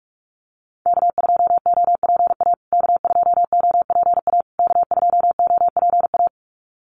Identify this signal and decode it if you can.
Morse code, heard as a keyed tone, about 44 wpm, 700 Hz. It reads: K2OPA K2OPA K2OPA
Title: USA Call Signs Repeated 3x 45wpm